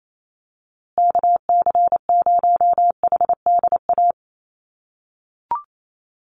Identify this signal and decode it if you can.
Morse code, heard as a keyed tone, about 28 wpm, 700 Hz. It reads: KC0HDA
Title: USA Call Signs Repeated 2x 28wpm